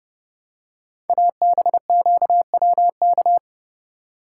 Morse code, heard as a keyed tone, about 30 wpm, 700 Hz. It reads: ABQWK
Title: Code Groups - 5 Letters - 30wpm